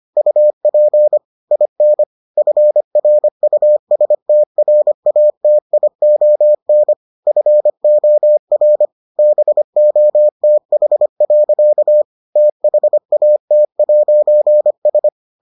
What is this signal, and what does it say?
Morse code, heard as a keyed tone, about 25 wpm, 600 Hz. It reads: UP IN FRUSTRATION FOR BOTH. THAT'S